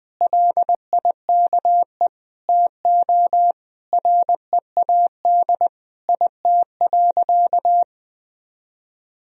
Morse code, heard as a keyed tone, about 20 wpm, 700 Hz. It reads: LIKE TO READ IT.
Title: Sentences from Top 500 Words and Abbreviations 20wpm